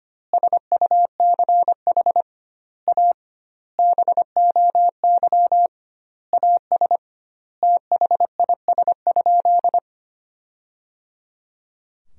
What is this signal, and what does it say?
Morse code, heard as a keyed tone, about 25 wpm, 700 Hz. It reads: SUCH A BOY AS THIS?
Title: 2x Word-Spacing - Sentences from Top 400 Words - 25wpm